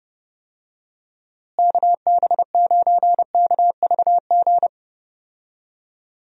Morse code, heard as a keyed tone, about 30 wpm, 700 Hz. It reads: KB9KVG
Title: USA Call Signs - 6 Characters - Repeated 3x - 30wpm